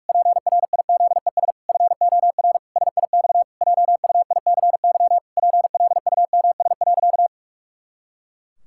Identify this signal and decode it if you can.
Morse code, heard as a keyed tone, about 46 wpm, 700 Hz. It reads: PRIZES FOR SIX JUICY PLUMS.
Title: Pangram Challenge v2 - 45wpm